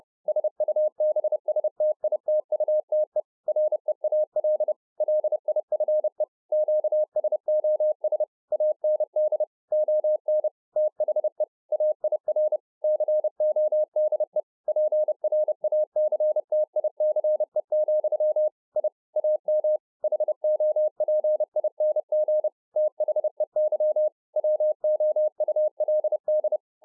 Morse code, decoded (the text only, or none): SUBSTITUTE REAL LIFE QSOS AND ON THE AIR CODE PRACTICE, I AM HOPING THEY WOULD